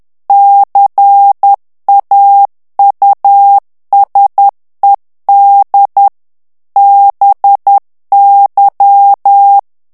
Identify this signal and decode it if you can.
Morse code, heard as a keyed tone, about 10 wpm, 800 Hz. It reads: CAUSED BY
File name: ANGLAIS texte clair 2 vitesse 10 mots minute